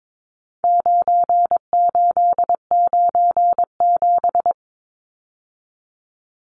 Morse code, heard as a keyed tone, about 22 wpm, 700 Hz. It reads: 9897
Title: Numbers - 4-Digit 22wpm